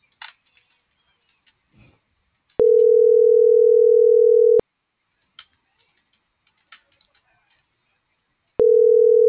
Ringback tone (North America), 440 + 480 Hz.